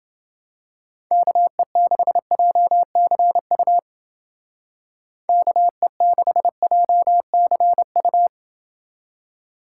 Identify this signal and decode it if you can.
Morse code, heard as a keyed tone, about 28 wpm, 700 Hz. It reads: KE6JCU KE6JCU